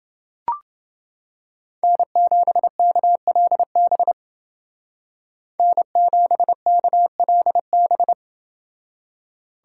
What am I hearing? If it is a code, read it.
Morse code, heard as a keyed tone, about 28 wpm, 700 Hz. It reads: N7KLB N7KLB